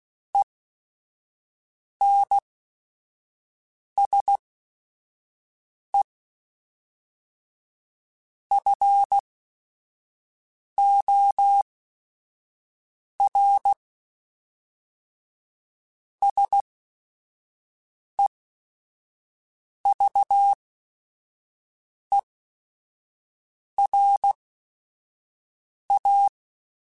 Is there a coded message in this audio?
Yes: Morse code from 0.3 to 26.3 s, heard as a keyed tone, about 16 wpm, spaced as for 5 wpm, 800 Hz. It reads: ENSE FOR SEVERA